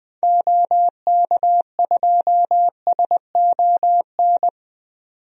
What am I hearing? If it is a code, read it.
Morse code, heard as a keyed tone, about 20 wpm, 700 Hz. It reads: OK2SON
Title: DXCC-100 Call Signs - 6 Characters - 20wpm